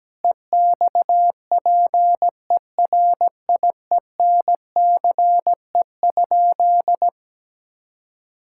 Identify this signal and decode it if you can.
Morse code, heard as a keyed tone, about 17 wpm, 700 Hz. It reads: EXPERIENCE?
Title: Sentences from Top 1000 Words - Review Part 5 - 17wpm